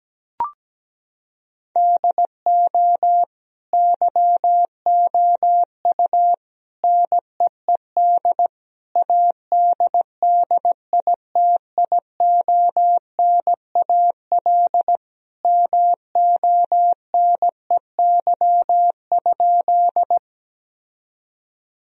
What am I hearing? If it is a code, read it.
Morse code, heard as a keyed tone, about 17 wpm, 700 Hz. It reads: DO YOU NEED ADDITIONAL MONEY?
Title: Sentences from Top 1000 Words - Review Part 4 - 17wpm